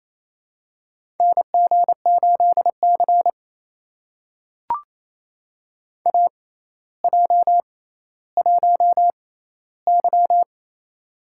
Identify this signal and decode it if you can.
Morse code, heard as a keyed tone, about 28 wpm, spaced as for 14 wpm, 700 Hz. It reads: NG8C AJ1Y
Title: USA Call Signs - 4 Characters - 6x Character Spacing - 28wpm